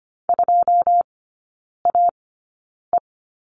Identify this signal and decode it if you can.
Morse code, heard as a keyed tone, about 25 wpm, spaced as for 9 wpm, 700 Hz. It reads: 2AE
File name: Lesson 17e - Letter u - Three Characters 25wpm